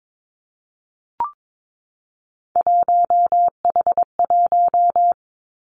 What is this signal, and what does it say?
Morse code, heard as a keyed tone, about 22 wpm, 700 Hz. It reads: 1H1
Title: Three Challenging Letter-Number Combinations 22wpm